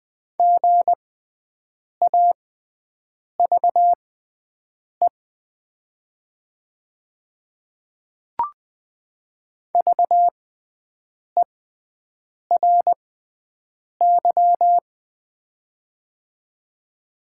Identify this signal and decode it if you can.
Morse code, heard as a keyed tone, about 20 wpm, spaced as for 7 wpm, 700 Hz. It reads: GAVE VERY
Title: 6x Character Spacing - Top 500 Words - 20wpm